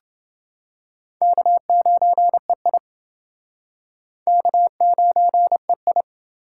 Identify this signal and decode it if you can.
Morse code, heard as a keyed tone, about 29 wpm, 700 Hz. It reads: K9EI K9EI